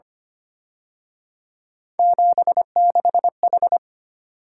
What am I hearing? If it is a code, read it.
Morse code, heard as a keyed tone, about 29 wpm, 700 Hz. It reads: 76H
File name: Speed-Racing - Three Letter-Number Combinations - 25wpm